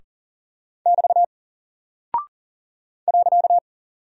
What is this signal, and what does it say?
Morse code, heard as a keyed tone, about 39 wpm, 700 Hz. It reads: <BT> .
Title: Single Character (Letter:Number:Punctuation:Prosign) Alt 40wpm